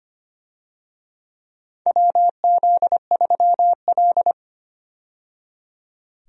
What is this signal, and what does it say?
Morse code, heard as a keyed tone, about 25 wpm, 700 Hz. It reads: WZ3L
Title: USA Call Signs - 4 Characters - 25wpm